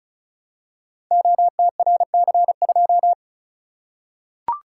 Morse code, heard as a keyed tone, about 36 wpm, 700 Hz. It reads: OTRC2
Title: Code Groups - 5 Letter:Numbers - 35wpm